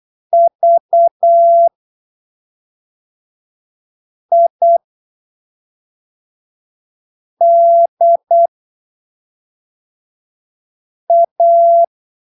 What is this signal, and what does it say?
Morse code, heard as a keyed tone, about 8 wpm, spaced as for 3 wpm, 680 Hz. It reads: VIDA
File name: arquivos / texto / texto 25 08ppm